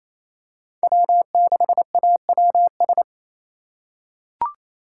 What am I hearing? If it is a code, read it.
Morse code, heard as a keyed tone, about 28 wpm, 700 Hz. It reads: W6AWS
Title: K1USN SST QSO Practice - 28wpm